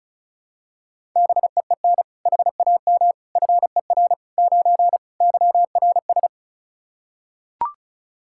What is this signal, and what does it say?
Morse code, heard as a keyed tone, about 36 wpm, 700 Hz. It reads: BEEN HAM FER 9 YRS
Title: Practice QSO - Exchange 3 - More Info - 35wpm